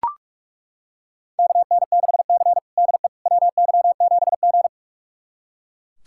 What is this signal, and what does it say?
Morse code, heard as a keyed tone, about 41 wpm, 700 Hz. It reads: KN6C DE WY7G
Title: USA Callsign DE Callsign - 4 Characters - 45wpm